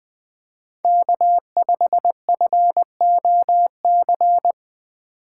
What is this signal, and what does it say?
Morse code, heard as a keyed tone, about 20 wpm, 700 Hz. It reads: K5FOC
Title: USA Call Signs - 5 Characters - 6x Character Spacing - 20wpm